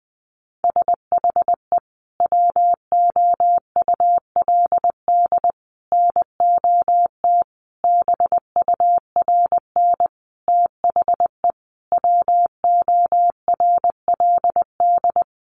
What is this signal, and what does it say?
Morse code, heard as a keyed tone, about 20 wpm, 700 Hz. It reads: SHE WOULD NOT BURN THE WORLD
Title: Sentences from Top 1300 Words 20wpm